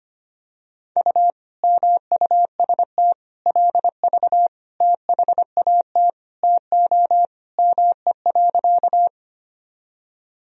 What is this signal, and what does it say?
Morse code, heard as a keyed tone, about 25 wpm, 700 Hz. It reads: U MUST LV THAT TO ME.
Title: Sentences from Top 400 Words and Abbreviations 25wpm